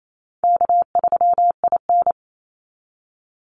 Morse code, heard as a keyed tone, about 28 wpm, 700 Hz. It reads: K3IN
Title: Easy Call Signs - 28wpm